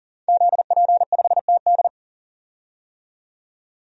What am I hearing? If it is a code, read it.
Morse code, heard as a keyed tone, about 40 wpm, 700 Hz. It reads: ZP5TD
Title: DXCC-100 Call Signs - 5 Characters - 40wpm